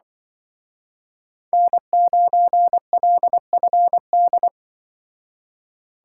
Morse code, heard as a keyed tone, about 25 wpm, 700 Hz. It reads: N9LFD
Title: Speed Racing - USA 1x3 Call Signs 20wpm